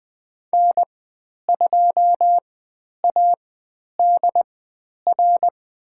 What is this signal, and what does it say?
Morse code, heard as a keyed tone, about 20 wpm, spaced as for 10 wpm, 700 Hz. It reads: N2ADR
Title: 095 Lesson 16g - Letter c - Call Signs 20wpm